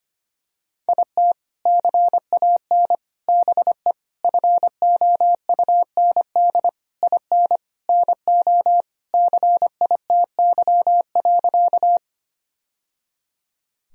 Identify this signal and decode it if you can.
Morse code, heard as a keyed tone, about 25 wpm, 700 Hz. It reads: IT CAN BE FOUND IN NO CITY.
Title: Sentences from Top 1000 Words - Review Part 1 - 25wpm